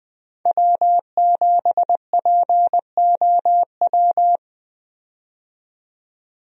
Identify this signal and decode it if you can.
Morse code, heard as a keyed tone, about 20 wpm, 700 Hz. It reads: W7POW
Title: USA Call Signs - 5 Characters - 20wpm